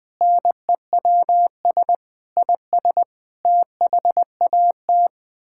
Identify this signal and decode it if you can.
Morse code, heard as a keyed tone, about 21 wpm, 700 Hz. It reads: NEWS IS THAT
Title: Build A Phrase - 5-Word Phrases - 20wpm